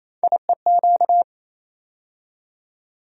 Morse code, heard as a keyed tone, about 27 wpm, 700 Hz. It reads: IEQ
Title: USA Call Sign Postfixes - 3 Letters 28wpm